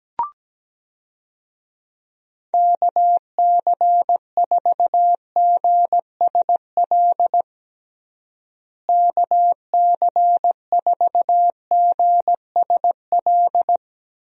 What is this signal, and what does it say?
Morse code, heard as a keyed tone, about 17 wpm, 700 Hz. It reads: KC4GSL KC4GSL
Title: USA Call Signs - 6 Characters - Repeated 2x - 17wpm